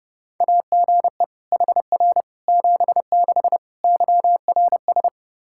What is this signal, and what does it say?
Morse code, heard as a keyed tone, about 32 wpm, 700 Hz. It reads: AGE HR 76 YRS
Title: Practice QSO - Exchange 3 - More Info - 30wpm